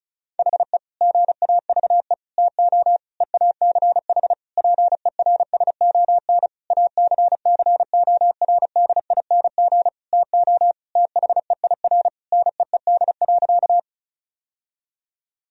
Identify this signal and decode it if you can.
Morse code, heard as a keyed tone, about 36 wpm, 700 Hz. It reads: HE GAVE TO EACH PERSON ACCORDING TO THEIR NEED.
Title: Sentences from Top 1000 Words - Review Part 4 - 35wpm